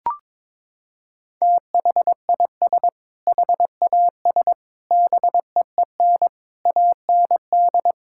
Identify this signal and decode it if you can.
Morse code, heard as a keyed tone, about 22 wpm, 700 Hz. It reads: THIS HAS BEEN AND
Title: Sentences from Top 1600 Words 22wpm